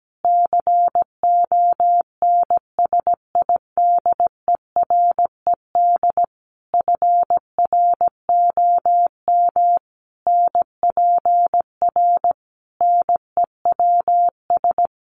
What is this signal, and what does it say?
Morse code, heard as a keyed tone, about 17 wpm, 700 Hz. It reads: CONSIDERED FROM NPR NEWS
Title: Sets of 5 Words Encore 17wpm